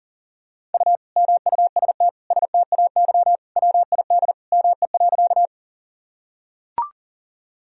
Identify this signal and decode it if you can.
Morse code, heard as a keyed tone, about 40 wpm, 700 Hz. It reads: U MUST STAY WID ME.
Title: Sentences from Top 500 Words and Abbreviations 40wpm